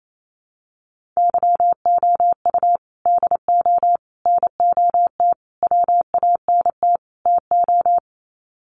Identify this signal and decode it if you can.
Morse code, heard as a keyed tone, about 28 wpm, 700 Hz. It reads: YOU DO NOT WANT TO
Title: Sets of 5 Words 28wpm